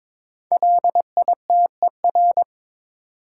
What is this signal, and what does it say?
Morse code, heard as a keyed tone, about 23 wpm, 700 Hz. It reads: LITER
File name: 5 Letter Words - First 3 Letters Given - 22wpm